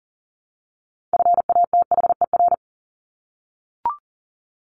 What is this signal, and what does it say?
Morse code, heard as a keyed tone, about 40 wpm, 700 Hz. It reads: FATHER
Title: Top 200 Words Repeated 40wpm